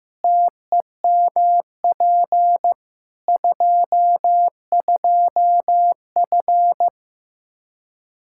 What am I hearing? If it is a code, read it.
Morse code, heard as a keyed tone, about 16 wpm, 700 Hz. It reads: TEMP 22F
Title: Practice QSO - Exchange 2 - Info - 15wpm